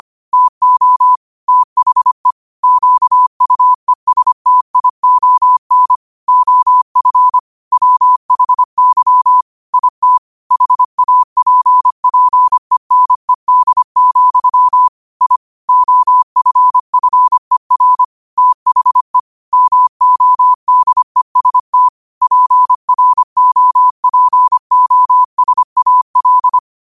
Morse code, heard as a keyed tone, about 25 wpm, 1000 Hz. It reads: TO THE QUESTION OF WHY IT HAPPENED, I OFFER THE MODEST PROPOSAL
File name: qotdfiles / 25wpm